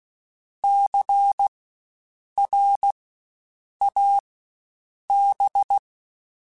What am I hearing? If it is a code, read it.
Morse code, heard as a keyed tone, about 16 wpm, spaced as for 7 wpm, 800 Hz. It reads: CRAB
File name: Spaced Alphas / Spaced Alphas mp3s / Cs sp @15